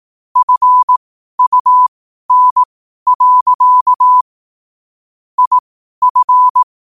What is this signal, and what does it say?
Morse code, heard as a keyed tone, about 18 wpm, spaced as for 12 wpm, 1000 Hz. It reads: FUN. IF
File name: qotdfiles / 12wpm